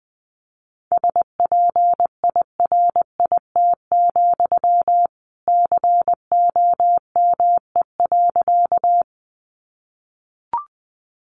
Morse code, heard as a keyed tone, about 20 wpm, 700 Hz. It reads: SPIRIT, COME.
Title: Sentences from Top 1400 Words 20wpm